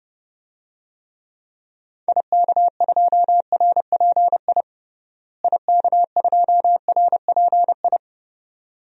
Morse code, heard as a keyed tone, about 30 wpm, 700 Hz. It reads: IK2RPI IK2RPI
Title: DXCC-100 Call Signs - 6 Characters - Repeated 2x - 30wpm